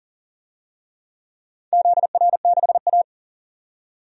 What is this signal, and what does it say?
Morse code, heard as a keyed tone, about 40 wpm, 700 Hz. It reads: ZR6A
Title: DXCC-100 Call Signs - 4 Characters - 40wpm